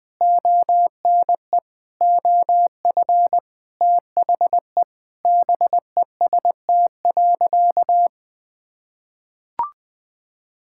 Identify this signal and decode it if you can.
Morse code, heard as a keyed tone, about 20 wpm, 700 Hz. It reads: ONE OF THE BEST.
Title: Sentences with 4-Word Phrases - 20wpm